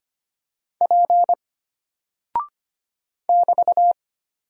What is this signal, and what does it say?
Morse code, heard as a keyed tone, about 25 wpm, 700 Hz. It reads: P <BT>
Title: Single Character (Letter:Number:Punctuation:Prosign) Alt 25wpm